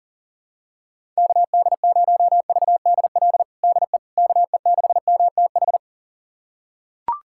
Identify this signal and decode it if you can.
Morse code, heard as a keyed tone, about 40 wpm, 700 Hz. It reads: KD0VDL DE KE6MTH
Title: USA Callsign DE Callsign - 6 Characters - 40wpm